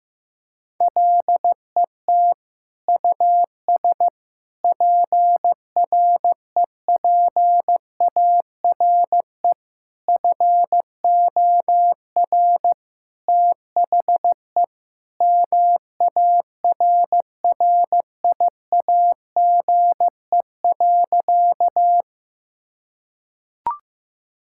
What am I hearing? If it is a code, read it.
Morse code, heard as a keyed tone, about 15 wpm, 700 Hz. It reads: LET US PREPARE FOR THE MARRIAGE.
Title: Sentences from Top 1000 Words - Review Part 6 - 15wpm